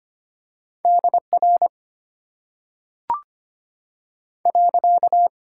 Morse code, heard as a keyed tone, about 25 wpm, 700 Hz. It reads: DR .
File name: Top 90 QSO Elements - 8x Character Spacing - 25wpm